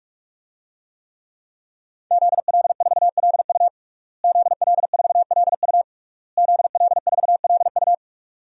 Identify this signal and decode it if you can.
Morse code, heard as a keyed tone, about 46 wpm, 700 Hz. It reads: ZL4LU ZL4LU ZL4LU
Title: DXCC-100 Call Signs - 5 Characters - Repeated 3x - 45wpm